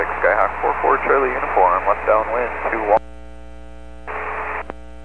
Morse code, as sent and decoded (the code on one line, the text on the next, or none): none